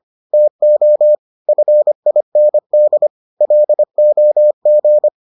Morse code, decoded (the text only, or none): TO FIND LOG